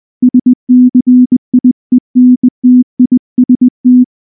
Morse code, heard as a keyed tone, about 19 wpm, 250 Hz. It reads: SCIENTIST
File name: morse / morse-training-20WPM-3